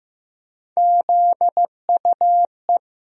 Morse code, heard as a keyed tone, about 15 wpm, 700 Hz. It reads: ZUE
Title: USA Call Sign Postfixes - 3 Letters 15wpm